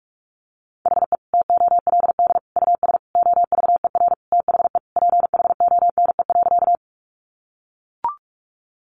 Morse code, heard as a keyed tone, about 44 wpm, 700 Hz. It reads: HE TOLD US OVER THE PHONE.
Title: Sentences from Top 1100 Words 45wpm